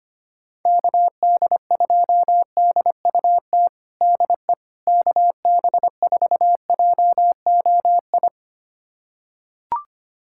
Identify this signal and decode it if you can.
Morse code, heard as a keyed tone, about 25 wpm, 700 Hz. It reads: KD2DUT DE KB4JOI
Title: USA Callsign DE Callsign - 6 Characters - 25wpm